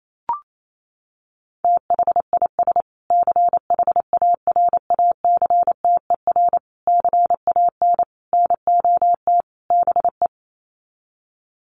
Morse code, heard as a keyed tone, about 28 wpm, 700 Hz. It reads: THIS CHARACTER CAN NOT BE
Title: Sets of 5 Words Encore2 28wpm